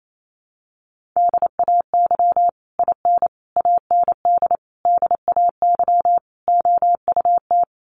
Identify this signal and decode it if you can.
Morse code, heard as a keyed tone, about 28 wpm, 700 Hz. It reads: DAY IN AND DAY OUT
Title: Sets of 5 Words Encore2 28wpm